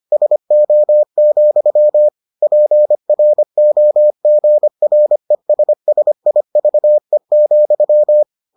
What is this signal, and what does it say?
Morse code, heard as a keyed tone, about 25 wpm, 600 Hz. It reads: SO, PROGRESSIVE,